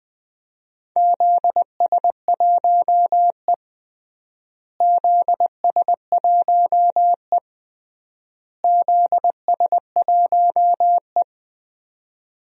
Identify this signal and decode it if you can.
Morse code, heard as a keyed tone, about 20 wpm, 700 Hz. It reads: ZS1E ZS1E ZS1E
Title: DXCC-100 Call Signs - 4 Characters - Repeated 3x - 20wpm